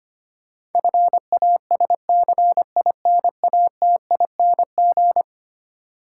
Morse code, heard as a keyed tone, about 25 wpm, 700 Hz. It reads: FASCINATING
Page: morse-code-ninja-practice-25wpm